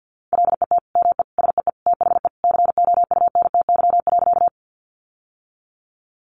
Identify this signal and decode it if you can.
Morse code, heard as a keyed tone, about 50 wpm, 700 Hz. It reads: LET ME SEE THE COUNTY.
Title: Sentences from Top 1200 Words 50wpm